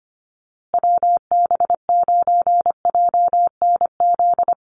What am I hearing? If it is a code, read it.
Morse code, heard as a keyed tone, about 25 wpm, 700 Hz. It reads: WB9JNZ
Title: USA Call Signs Repeated 2x 25wpm